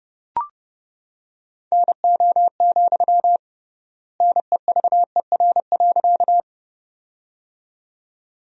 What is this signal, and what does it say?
Morse code, heard as a keyed tone, about 31 wpm, 700 Hz. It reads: NO, NEVER.
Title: 3x Word-Spacing - Sentences from Top 300 Words - 30wpm